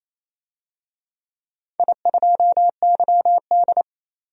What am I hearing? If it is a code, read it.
Morse code, heard as a keyed tone, about 28 wpm, 700 Hz. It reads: I2YD
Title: DXCC-100 Call Signs - 4 Characters - 28wpm